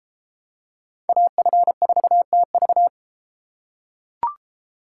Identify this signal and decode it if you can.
Morse code, heard as a keyed tone, about 31 wpm, 700 Hz. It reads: AF4TV